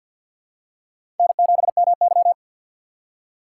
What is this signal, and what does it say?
Morse code, heard as a keyed tone, about 51 wpm, 700 Hz. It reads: N7MY
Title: USA Call Signs - 4 Characters - 6x Character Spacing - 50wpm